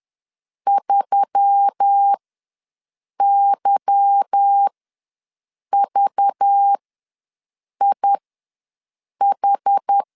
Morse code, heard as a keyed tone, about 10 wpm, spaced as for 6 wpm, 800 Hz. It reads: 3YVIH
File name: LEZIONI / 9 28 ws 10wpm 4124